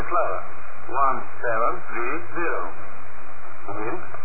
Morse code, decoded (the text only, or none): none